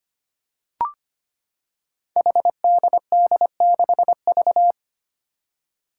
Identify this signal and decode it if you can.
Morse code, heard as a keyed tone, about 25 wpm, 700 Hz. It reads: HDD6V
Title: Five Challenging Letter-Number Combinations 25wpm